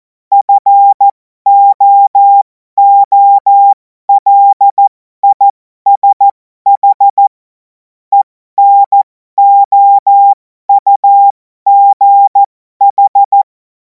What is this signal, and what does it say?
Morse code, heard as a keyed tone, about 14 wpm, spaced as for 12 wpm, 800 Hz. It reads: FOOLISH ENOUGH
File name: Morsekurs / Witze